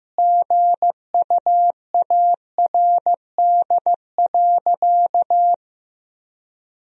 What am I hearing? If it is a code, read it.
Morse code, heard as a keyed tone, about 16 wpm, 700 Hz. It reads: GUARD.
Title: Sentences from Top 1000 Words - Review Part 4 - 15wpm